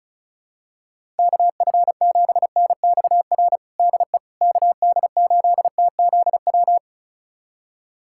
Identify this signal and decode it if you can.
Morse code, heard as a keyed tone, about 36 wpm, 700 Hz. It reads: KF7NXR DE KD8TZW
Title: USA Callsign DE Callsign - 6 Characters - 35wpm